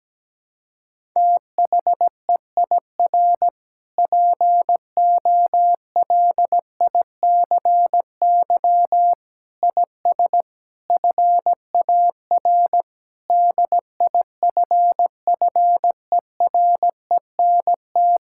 Morse code, heard as a keyed tone, about 17 wpm, 700 Hz. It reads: THEIR POLICY IS FAR DIFFERENT
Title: Sentences from Top 1000 Words - Review Part 5 - 17wpm